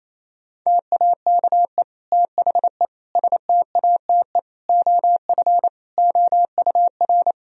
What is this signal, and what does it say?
Morse code, heard as a keyed tone, about 28 wpm, 700 Hz. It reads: TAKE THE STATE OF OUR
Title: Sentences from Top 1000 Words - Review Part 3 - 28wpm